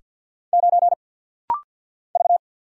Morse code, heard as a keyed tone, about 48 wpm, 700 Hz. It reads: 9 V
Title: Single Letter-Number - Rapid-Fire 50wpm